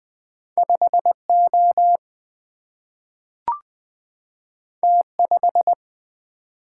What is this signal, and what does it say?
Morse code, heard as a keyed tone, about 20 wpm, 700 Hz. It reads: 5 O T 5
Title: Single-Letter Single-Number 20wpm